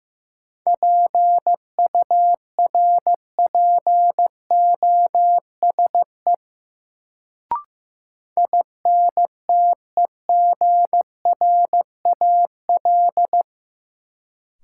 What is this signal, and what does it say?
Morse code, heard as a keyed tone, about 15 wpm, 700 Hz. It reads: PURPOSE INTEGRAL
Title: Sets of 5 Words - New Word Drill - 15wpm